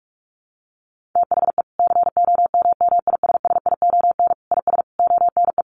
Morse code, heard as a keyed tone, about 44 wpm, 700 Hz. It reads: THE COMMISSION IS ONE
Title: Sentences from Top 1400 Words 45wpm